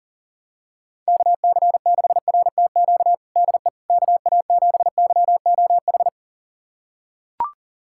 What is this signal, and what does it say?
Morse code, heard as a keyed tone, about 40 wpm, 700 Hz. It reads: KC6RTQ DE KA7YOH
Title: USA Callsign DE Callsign - 6 Characters - 40wpm